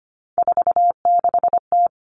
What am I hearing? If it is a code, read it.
Morse code, heard as a keyed tone, about 25 wpm, 700 Hz. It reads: 46T
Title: All Cut Numbers - 3-Digit 25wpm